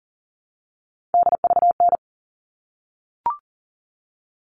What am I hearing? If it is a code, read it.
Morse code, heard as a keyed tone, about 40 wpm, 700 Hz. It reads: DVN
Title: All Cut Numbers - 3-Digit 40wpm